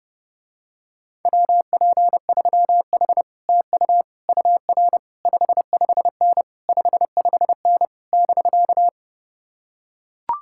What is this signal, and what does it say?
Morse code, heard as a keyed tone, about 31 wpm, 700 Hz. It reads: WP3H TU UR 55N 55N <BK>